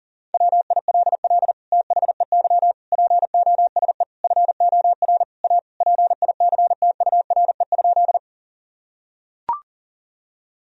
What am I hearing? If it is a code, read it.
Morse code, heard as a keyed tone, about 40 wpm, 700 Hz. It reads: WILL THEY POSE FOR A PICTURE?
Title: Sentences from Top 1000 Words - Review Part 2 - 40wpm